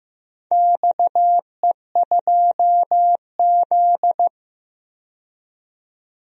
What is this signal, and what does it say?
Morse code, heard as a keyed tone, about 15 wpm, 700 Hz. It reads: XE2Z
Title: DXCC-100 Call Signs - 4 Characters - 15wpm